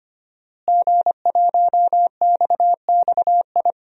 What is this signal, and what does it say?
Morse code, heard as a keyed tone, about 25 wpm, 700 Hz. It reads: G1XXI
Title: DXCC-100 Call Signs - 5 Characters - Repeated 2x - 25wpm